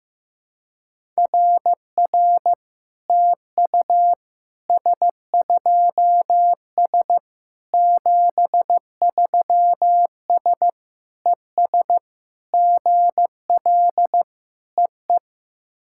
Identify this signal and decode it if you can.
Morse code, heard as a keyed tone, about 15 wpm, 700 Hz. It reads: RR TU S2S 73S ES GL EE
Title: SOTA QSO Practice 15wpm